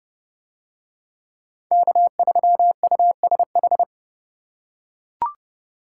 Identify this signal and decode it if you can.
Morse code, heard as a keyed tone, about 29 wpm, 700 Hz. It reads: K3USH